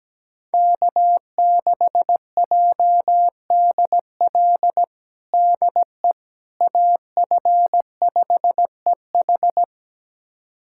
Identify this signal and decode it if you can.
Morse code, heard as a keyed tone, about 17 wpm, 700 Hz. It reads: K6JDL DE AF5EH
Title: USA Callsign DE Callsign - 5 Characters - 17wpm